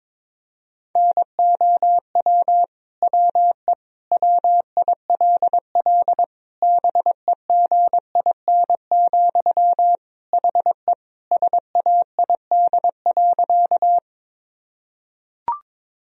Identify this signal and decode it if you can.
Morse code, heard as a keyed tone, about 21 wpm, 700 Hz. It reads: NOW WE WILL BEGIN, HE SAID.
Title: Sentences from Top 1000 Words - Review Part 1 - 22wpm